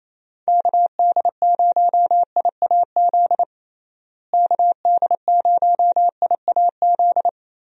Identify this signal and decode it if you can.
Morse code, heard as a keyed tone, about 27 wpm, 700 Hz. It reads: KD0IAZ KD0IAZ
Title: USA Call Signs - 6 Characters - Repeated 2x - 28wpm